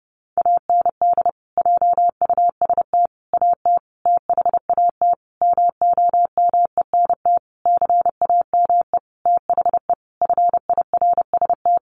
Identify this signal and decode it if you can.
Morse code, heard as a keyed tone, about 30 wpm, 700 Hz. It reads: AND JUST AT THAT MOMENT CAME THE FIRST
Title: Sentences from Top 1500 Words 30wpm